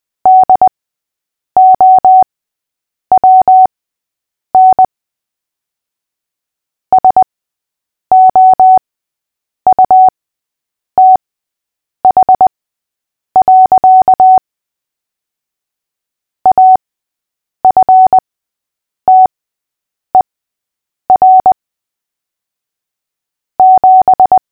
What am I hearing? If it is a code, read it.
Morse code, heard as a keyed tone, about 20 wpm, spaced as for 8 wpm, 750 Hz. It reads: DOWN SOUTH. AFTER 7
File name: cw / fists / Short-Story-5-8wpm